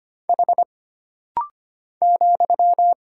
Morse code, heard as a keyed tone, about 25 wpm, 700 Hz. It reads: H ,
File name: Single Character (Letter:Number:Punctuation:Prosign) - Rapid-Fire - 25wpm